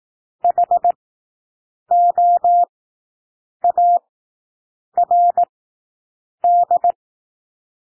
Morse code, heard as a keyed tone, about 18 wpm, spaced as for 7 wpm, 700 Hz. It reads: HOARD
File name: SOUND WORDS 18 wpm